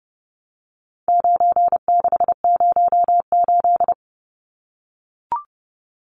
Morse code, heard as a keyed tone, about 30 wpm, 700 Hz. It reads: 9608
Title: Numbers - 4-Digit 30wpm